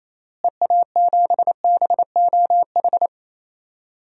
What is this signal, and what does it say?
Morse code, heard as a keyed tone, about 28 wpm, 700 Hz. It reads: EA7BOH
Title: DXCC-100 Call Signs - 6 Characters - Repeated 3x - 28wpm